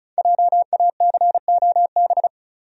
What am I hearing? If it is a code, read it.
Morse code, heard as a keyed tone, about 36 wpm, 700 Hz. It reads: JACOB
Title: USA Common Names - 35wpm